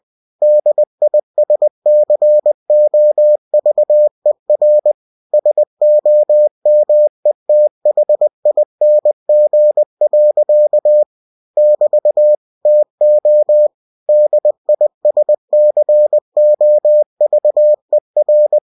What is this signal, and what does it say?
Morse code, heard as a keyed tone, about 20 wpm, 600 Hz. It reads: DISCOVER SOMETHING. <BT> TO DISCOVER
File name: cw-books / winnie-the-pooh / winnie-20wpm-page-0079